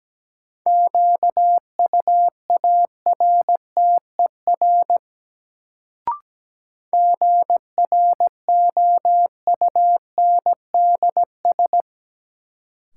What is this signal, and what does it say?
Morse code, heard as a keyed tone, about 17 wpm, 700 Hz. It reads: QUARTER GROUNDS
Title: Sentences from Top 1300 Words - New Word Drill Alt - 17wpm